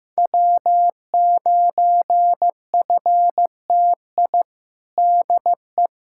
Morse code, heard as a keyed tone, about 15 wpm, 700 Hz. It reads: W9FTI DE
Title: K1USN SST QSO Practice 15wpm